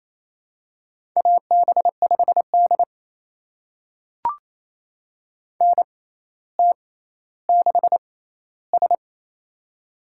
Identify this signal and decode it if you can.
Morse code, heard as a keyed tone, about 27 wpm, spaced as for 14 wpm, 700 Hz. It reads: AB5D NT6S
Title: USA Call Signs - 4 Characters - 6x Character Spacing - 28wpm